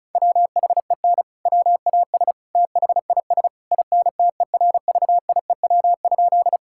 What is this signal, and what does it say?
Morse code, heard as a keyed tone, about 36 wpm, 700 Hz. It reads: WHEN WAS THIS INTERVIEW?
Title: Sentences from Top 1600 Words 35wpm